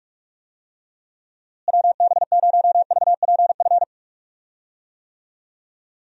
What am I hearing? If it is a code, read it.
Morse code, heard as a keyed tone, about 46 wpm, 700 Hz. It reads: WB0VPF
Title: USA Call Signs - 6 Characters - Repeated 3x - 45wpm